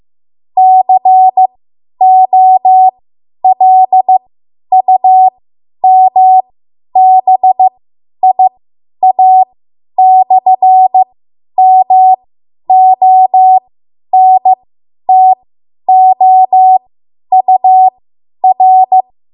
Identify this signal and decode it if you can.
Morse code, heard as a keyed tone, about 15 wpm, spaced as for 10 wpm, 750 Hz. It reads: COLUMBIA/MONTOUR
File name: Morse / 10WPM